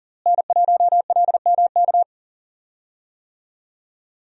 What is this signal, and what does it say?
Morse code, heard as a keyed tone, about 40 wpm, 700 Hz. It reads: N1LMK
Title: USA Call Signs - 5 Characters - Repeated 2x - 40wpm